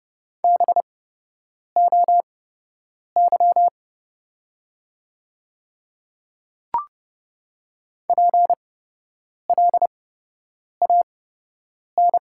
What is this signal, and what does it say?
Morse code, heard as a keyed tone, about 30 wpm, spaced as for 8 wpm, 700 Hz. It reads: BOY PLAN
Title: 8x Character Spacing - Top 500 Words - 30wpm